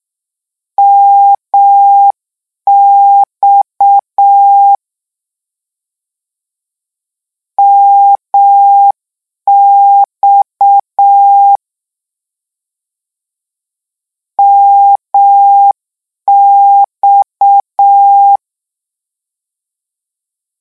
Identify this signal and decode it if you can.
Morse code, heard as a keyed tone, about 6 wpm, 800 Hz. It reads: MX MX MX
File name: morse mp3 / lecon 1-vitesse 6